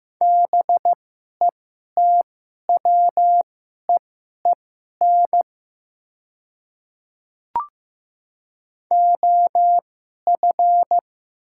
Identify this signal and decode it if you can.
Morse code, heard as a keyed tone, about 15 wpm, spaced as for 11 wpm, 700 Hz. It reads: BETWEEN OF
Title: 2x Character Spacing - Top 500 Words - 15wpm